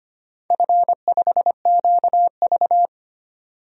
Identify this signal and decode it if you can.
Morse code, heard as a keyed tone, about 25 wpm, 690 Hz. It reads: F5QV
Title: DXCC-100 Call Signs - 4 Characters - 25wpm